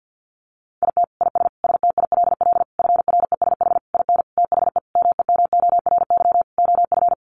Morse code, heard as a keyed tone, about 50 wpm, 700 Hz. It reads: IT IS STILL FRESH IN THE MEMORY OF
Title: Sentences from Top 1300 Words 50wpm